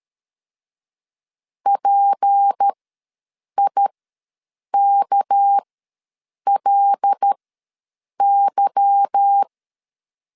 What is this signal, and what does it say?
Morse code, heard as a keyed tone, about 13 wpm, spaced as for 7 wpm, 800 Hz. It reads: PIKLY
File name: LEZIONI / 9 286 ws 12wpm 3351